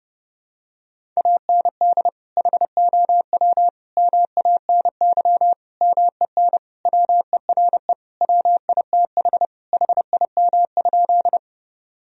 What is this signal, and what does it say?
Morse code, heard as a keyed tone, about 30 wpm, 700 Hz. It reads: AND HOW MANY MEN WERE WITH HIM?
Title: Sentences from Top 1000 Words - Review Part 1 - 30wpm